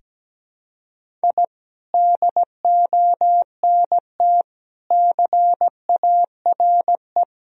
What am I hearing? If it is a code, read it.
Morse code, heard as a keyed tone, about 17 wpm, 700 Hz. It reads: I DONT CARE
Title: Sentences from Top 1000 Words - Review Part 1 - 17wpm